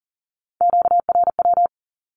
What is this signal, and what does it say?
Morse code, heard as a keyed tone, about 42 wpm, 700 Hz. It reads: QRW
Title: Q-Codes with Spoken Meaning - 40wpm